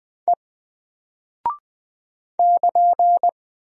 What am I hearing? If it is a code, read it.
Morse code, heard as a keyed tone, about 20 wpm, 700 Hz. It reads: E <KN>
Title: Single Character (Letter:Number:Punctuation:Prosign) Alt 20wpm